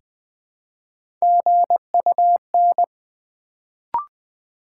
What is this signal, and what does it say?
Morse code, heard as a keyed tone, about 20 wpm, 700 Hz. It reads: GUN
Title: Build A Word - 1 Letter At A Time - 3-Letter Words - 20wpm